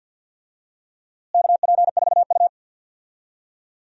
Morse code, heard as a keyed tone, about 47 wpm, 700 Hz. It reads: KP4U
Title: DXCC-100 Call Signs - 4 Characters - Repeated 2x - 50wpm